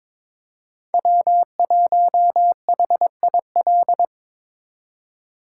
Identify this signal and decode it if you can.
Morse code, heard as a keyed tone, about 22 wpm, 700 Hz. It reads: W1HIL
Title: USA Call Signs - 5 Characters - 4x Character Spacing - 22wpm